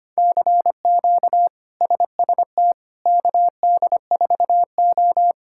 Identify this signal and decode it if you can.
Morse code, heard as a keyed tone, about 25 wpm, 700 Hz. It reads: CQ SST KD4O
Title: K1USN SST QSO Practice - 25wpm